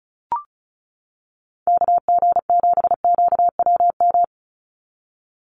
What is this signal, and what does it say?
Morse code, heard as a keyed tone, about 34 wpm, 700 Hz. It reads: KG7QWM
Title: USA Call Signs 35wpm